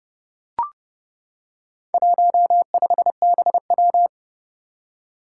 Morse code, heard as a keyed tone, about 30 wpm, 700 Hz. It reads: 15BW
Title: Four Challenging Letter-Number Combinations 30wpm